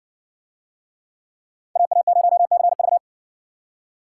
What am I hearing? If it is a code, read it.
Morse code, heard as a keyed tone, about 58 wpm, 700 Hz. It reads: AA0CV